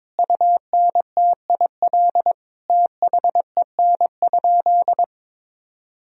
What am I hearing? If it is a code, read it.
Morse code, heard as a keyed tone, about 22 wpm, 700 Hz. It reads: UNTIL THEN?
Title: Sentences from Top 500 Words and Abbreviations 22wpm